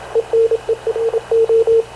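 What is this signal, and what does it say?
Morse code, heard as a keyed tone, about 27 wpm, 450 Hz, with radio noise. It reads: ENERO
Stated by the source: Havana Radio / CLA sending Spanish press (PX) in Morse, recorded by Radio Officer Paul Dumesnil onboard CNAV QUEST/CZDO.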